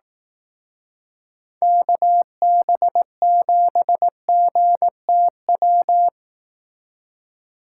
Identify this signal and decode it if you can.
Morse code, heard as a keyed tone, about 18 wpm, 700 Hz. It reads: KB7GTW
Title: Speed Racing - USA 2x3 Call Signs 15wpm